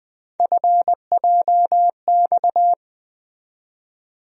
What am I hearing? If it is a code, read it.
Morse code, heard as a keyed tone, about 20 wpm, 700 Hz. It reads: FJX
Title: USA Call Sign Postfixes - 3 Letters 20wpm